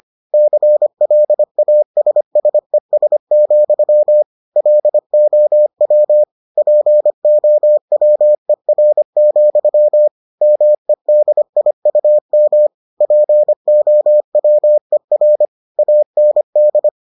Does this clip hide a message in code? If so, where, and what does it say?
Yes: Morse code from 0.3 to 16.9 s, heard as a keyed tone, about 25 wpm, 600 Hz. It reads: CLASSES, LOW POWER, MEDIUM POWER AND